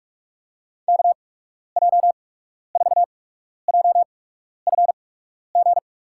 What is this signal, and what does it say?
Morse code, heard as a keyed tone, about 43 wpm, spaced as for 12 wpm, 700 Hz. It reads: KJ4JFG
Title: USA Call Signs - 8x Character Spacing - 45wpm